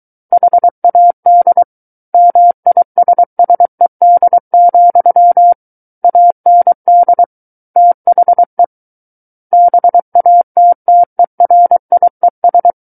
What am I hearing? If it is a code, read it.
Morse code, heard as a keyed tone, about 23 wpm, 700 Hz. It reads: HAD MISSED, AND THE BATTERIES
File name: Book One / 14-Chapter 14 - 23 WPM